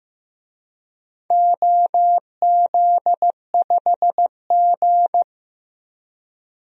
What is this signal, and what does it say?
Morse code, heard as a keyed tone, about 15 wpm, 700 Hz. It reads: OZ5G
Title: DXCC-100 Call Signs - 4 Characters - 15wpm